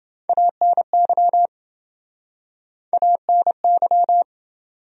Morse code, heard as a keyed tone, about 30 wpm, 700 Hz. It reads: ANY ANY